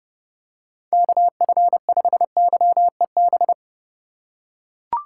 Morse code, heard as a keyed tone, about 31 wpm, 700 Hz. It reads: KF5YEB
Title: USA Call Signs - 6 Characters - 2x Character Spacing - 30wpm